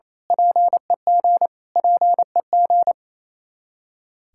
Morse code, heard as a keyed tone, about 28 wpm, 700 Hz. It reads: PEG PEG
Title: Practice QSO - Exchange 1 - Beginning - 28wpm